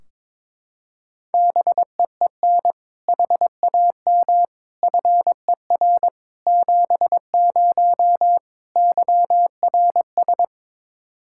Morse code, heard as a keyed tone, about 22 wpm, 700 Hz. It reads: BEEN HAM FER 70 YRS
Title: Practice QSO - Exchange 3 - More Info - 22wpm